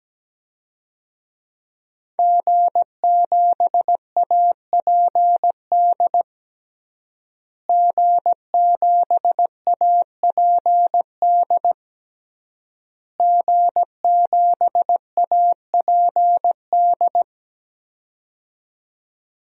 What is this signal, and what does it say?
Morse code, heard as a keyed tone, about 17 wpm, 700 Hz. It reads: G7APD G7APD G7APD
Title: DXCC-100 Call Signs - 5 Characters - Repeated 3x - 17wpm